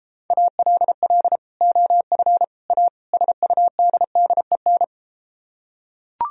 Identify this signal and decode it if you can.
Morse code, heard as a keyed tone, about 34 wpm, 700 Hz. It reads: ALL OF A SUDDEN